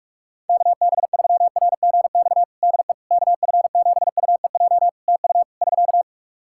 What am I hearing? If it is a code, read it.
Morse code, heard as a keyed tone, about 46 wpm, 700 Hz. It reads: KB3RGX DE KF7UEJ TU <SK>
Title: Practice QSO - Exchange 4 - Ending - 45wpm